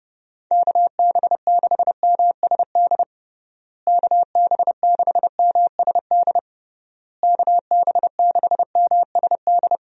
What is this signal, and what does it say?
Morse code, heard as a keyed tone, about 32 wpm, 700 Hz. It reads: KB6MSD KB6MSD KB6MSD
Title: USA Call Signs - 6 Characters - Repeated 3x - 30wpm